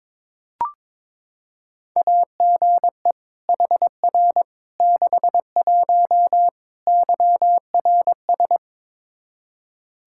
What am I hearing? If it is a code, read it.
Morse code, heard as a keyed tone, about 22 wpm, 700 Hz. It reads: AGE HR 61 YRS
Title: Practice QSO - Exchange 3 - More Info - 22wpm